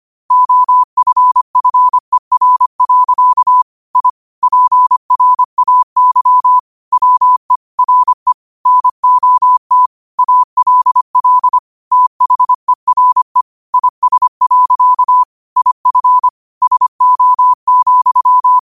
Morse code, heard as a keyed tone, about 25 wpm, 1000 Hz. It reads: OFFER. I PRAY WE RE NOT ALL THERE IS. IF SO,
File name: qotdfiles / 25wpm